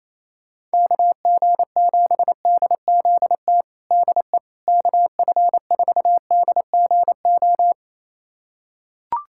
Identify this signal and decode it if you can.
Morse code, heard as a keyed tone, about 28 wpm, 700 Hz. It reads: KG7DZT DE KF4DGO
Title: USA Callsign DE Callsign - 6 Characters - 28wpm